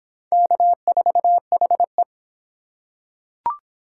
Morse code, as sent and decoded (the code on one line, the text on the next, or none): -.- ....- .... .
K4HE